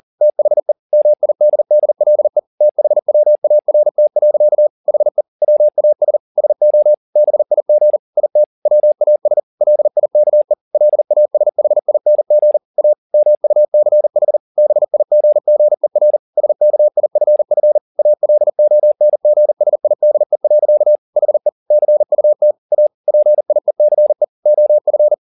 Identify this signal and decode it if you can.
Morse code, heard as a keyed tone, about 40 wpm, 600 Hz. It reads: THE MIDDLE THWART. HE WAS SO BIG IT WAS LIKE LASHING A MUCH BIGGER SKIFF ALONGSIDE. HE CUT A PIECE OF
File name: the old man and the sea 2 40wpm